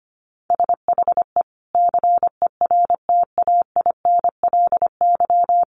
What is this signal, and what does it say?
Morse code, heard as a keyed tone, about 25 wpm, 700 Hz. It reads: SHE CERTAINLY
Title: Sentences from Top 1100 Words 25wpm